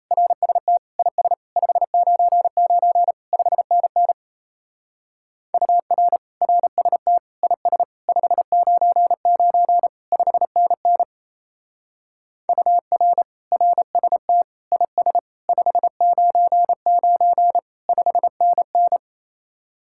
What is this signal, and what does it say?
Morse code, heard as a keyed tone, about 34 wpm, 700 Hz. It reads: RST IS 599 5NN UR RST IS 599 5NN UR RST IS 599 5NN
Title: Speed-Racing Common QSO Phrases Encore - 28wpm